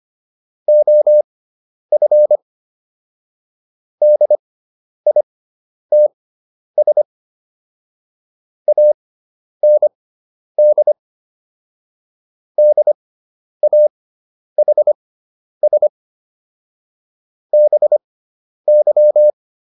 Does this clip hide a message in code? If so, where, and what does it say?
Yes: Morse code from 0.7 to 19.3 s, heard as a keyed tone, about 25 wpm, spaced as for 10 wpm, 600 Hz. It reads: OF DITS AND DAHS BY